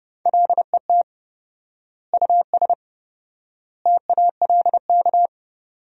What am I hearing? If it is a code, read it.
Morse code, heard as a keyed tone, about 30 wpm, 700 Hz. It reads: LET US TALK
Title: 4x Word-Spacing - Sentences from Top 400 Words - 30wpm